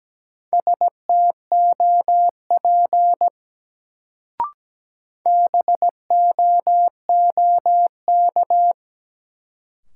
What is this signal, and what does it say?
Morse code, heard as a keyed tone, about 17 wpm, 700 Hz. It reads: STOP BOOK
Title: Sentences from Top 300 Words - New Word Drill - 17wpm